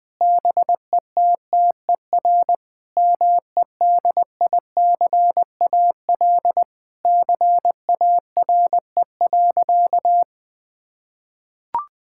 Morse code, heard as a keyed tone, about 20 wpm, 700 Hz. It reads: BETTER MEDICAL CARE.
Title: Sentences from Top 1000 Words - Review Part 5 - 20wpm